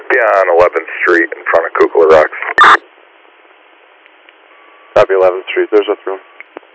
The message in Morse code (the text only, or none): none